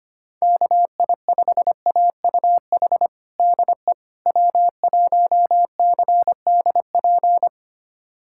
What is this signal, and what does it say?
Morse code, heard as a keyed tone, about 25 wpm, 700 Hz. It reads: KI5AUH DE W1CDP
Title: USA Callsign DE Callsign 25wpm